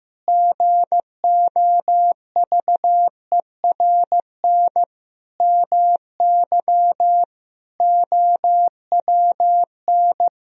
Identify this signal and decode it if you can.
Morse code, heard as a keyed tone, about 15 wpm, 700 Hz. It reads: GOVERN MY OWN
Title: Sentences from Top 1000 Words - Review Part 2 - 15wpm